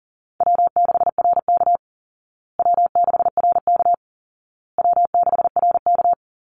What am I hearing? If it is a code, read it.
Morse code, heard as a keyed tone, about 40 wpm, 700 Hz. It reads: W6RK W6RK W6RK
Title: Call Signs Repeated 2x - 40wpm